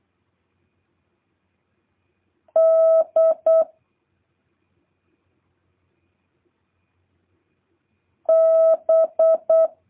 Morse code, heard as a keyed tone, about 8 wpm, 650 Hz. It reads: D B